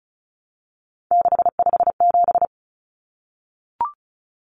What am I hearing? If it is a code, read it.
Morse code, heard as a keyed tone, about 34 wpm, 700 Hz. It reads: 657
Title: Common Cut Numbers - 3-Digit 35wpm